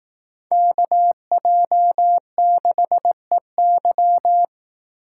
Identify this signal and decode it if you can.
Morse code, heard as a keyed tone, about 18 wpm, 700 Hz. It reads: KJ6EY
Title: Speed Racing - USA Call Signs 15wpm